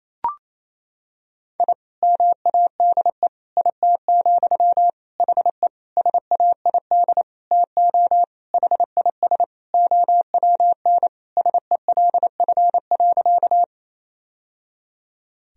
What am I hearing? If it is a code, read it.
Morse code, heard as a keyed tone, about 28 wpm, 700 Hz. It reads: I MADE IT, HE SAID TO HIS OWN SELF.
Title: Sentences from Top 1000 Words - Review Part 1 - 28wpm